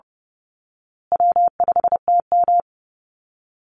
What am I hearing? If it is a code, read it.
Morse code, heard as a keyed tone, about 30 wpm, 700 Hz. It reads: W5TM
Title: Call Signs Repeated 3x - 30wpm